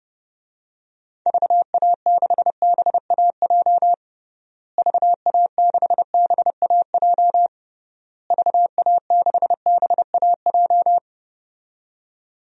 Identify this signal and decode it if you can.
Morse code, heard as a keyed tone, about 30 wpm, 700 Hz. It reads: VA6BAJ VA6BAJ VA6BAJ
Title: DXCC-100 Call Signs - 6 Characters - Repeated 3x - 30wpm